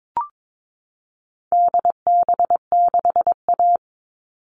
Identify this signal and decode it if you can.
Morse code, heard as a keyed tone, about 22 wpm, 700 Hz. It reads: DB6A
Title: All Cut Numbers - 4-Digit 22wpm